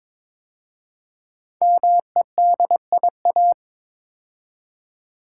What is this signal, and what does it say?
Morse code, heard as a keyed tone, about 22 wpm, 700 Hz. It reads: MEDIA
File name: 5 Letter Words - First 3 Letters Given - 22wpm